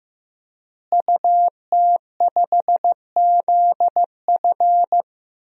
Morse code, heard as a keyed tone, about 15 wpm, 700 Hz. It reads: UT5ZF
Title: DXCC-100 Call Signs - 5 Characters - Repeated 3x - 15wpm